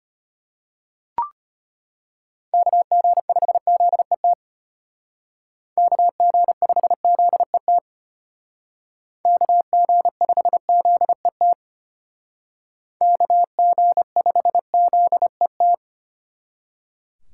Morse code, heard as a keyed tone, about 30 wpm, 700 Hz. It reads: KG5ZET KG5ZET KG5ZET KG5ZET
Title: Speed Racing - USA 2x3 Call Signs 25wpm